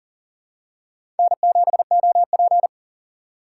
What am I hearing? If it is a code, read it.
Morse code, heard as a keyed tone, about 40 wpm, 700 Hz. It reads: N7OP
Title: USA Call Signs - 4 Characters - 4x Character Spacing - 40wpm